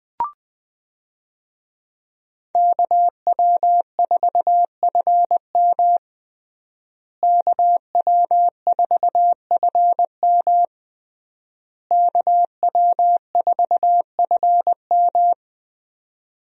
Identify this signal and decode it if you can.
Morse code, heard as a keyed tone, about 20 wpm, 700 Hz. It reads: KW4FM KW4FM KW4FM
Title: USA Call Signs - 5 Characters - Repeated 3x - 20wpm